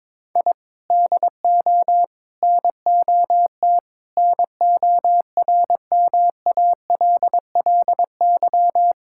Morse code, heard as a keyed tone, about 21 wpm, 700 Hz. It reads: I DO NOT NORMALLY
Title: Sentences from Top 1000 Words - Review Part 6 - 22wpm